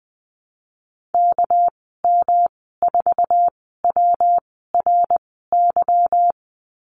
Morse code, heard as a keyed tone, about 20 wpm, spaced as for 15 wpm, 700 Hz. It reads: KM4WRY
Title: 219 Lesson 35g - Number 0 - Call Signs 20wpm